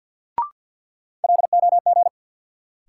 Morse code, heard as a keyed tone, about 52 wpm, 700 Hz. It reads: LOG
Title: Sentences from Top 1400 Words - New Word Drill - 50wpm